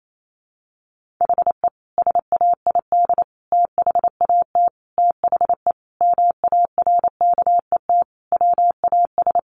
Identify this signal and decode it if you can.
Morse code, heard as a keyed tone, about 28 wpm, 700 Hz. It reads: HE SAID THAT THE MARKET WAS
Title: Sentences from Top 1400 Words 28wpm